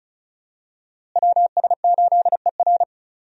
Morse code, heard as a keyed tone, about 34 wpm, 700 Hz. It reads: WS8ER
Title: USA Call Signs - 5 Characters - Repeated 3x - 35wpm